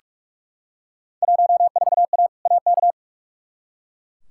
Morse code, heard as a keyed tone, about 44 wpm, 700 Hz. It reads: 14A AK
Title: Field Day Exchanges Encore - 45wpm